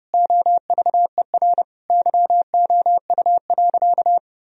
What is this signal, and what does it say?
Morse code, heard as a keyed tone, about 32 wpm, 700 Hz. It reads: OVER YOU.
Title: Sentences from Top 1000 Words - Review Part 1 - 30wpm